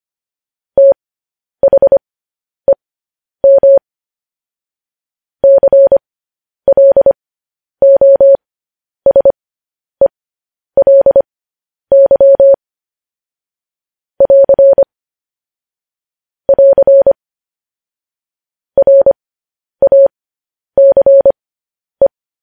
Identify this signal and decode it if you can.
Morse code, heard as a keyed tone, about 25 wpm, spaced as for 10 wpm, 550 Hz. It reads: THEM CLOSELY <AR> <AR> RACE
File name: podcast / News headlines-1x-25x10